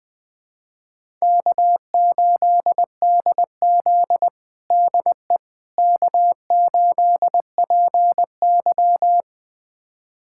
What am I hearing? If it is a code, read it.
Morse code, heard as a keyed tone, about 20 wpm, 700 Hz. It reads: K8DZ DE K8PY
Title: USA Callsign DE Callsign - 4 Characters - 20wpm